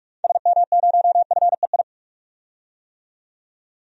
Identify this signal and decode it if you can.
Morse code, heard as a keyed tone, about 45 wpm, 700 Hz. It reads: SM0FEI
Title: DXCC-100 Call Signs - 6 Characters - 45wpm